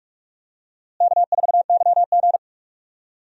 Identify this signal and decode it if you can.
Morse code, heard as a keyed tone, about 44 wpm, 700 Hz. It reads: K4YG
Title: USA Call Signs - 4 Characters - Repeated 2x - 45wpm